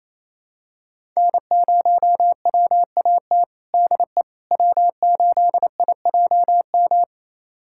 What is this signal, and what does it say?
Morse code, heard as a keyed tone, about 28 wpm, 700 Hz. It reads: N0WAT DE W8IJM
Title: USA Callsign DE Callsign - 5 Characters - 28wpm